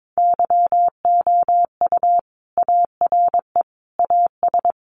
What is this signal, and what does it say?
Morse code, heard as a keyed tone, about 22 wpm, 700 Hz. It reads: YOU ARE AS
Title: Sentences from Top 1500 Words 22wpm